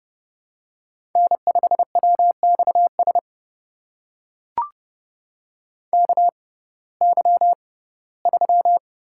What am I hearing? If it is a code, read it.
Morse code, heard as a keyed tone, about 30 wpm, spaced as for 18 wpm, 700 Hz. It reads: N5WXS KY3
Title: USA Call Signs - 6x Character Spacing - 30wpm